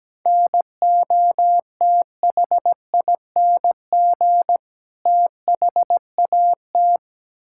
Morse code, heard as a keyed tone, about 17 wpm, 700 Hz. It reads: NOTHING THAT
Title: Sentences from Top 1000 Words - Review Part 6 - 17wpm